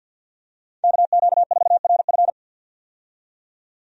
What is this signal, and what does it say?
Morse code, heard as a keyed tone, about 53 wpm, 700 Hz. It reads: KQ4RF
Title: USA Call Signs - 5 Characters - 50wpm